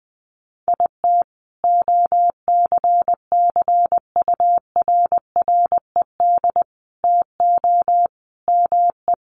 Morse code, heard as a keyed tone, about 20 wpm, 700 Hz. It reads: IT OCCURRED TO ME
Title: Sets of 5 Words Encore 20wpm